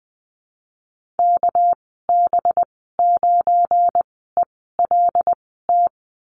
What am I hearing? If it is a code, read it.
Morse code, heard as a keyed tone, about 20 wpm, spaced as for 14 wpm, 700 Hz. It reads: KB9ELT
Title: 245 Lesson 39g - Letter z - Call Signs 20wpm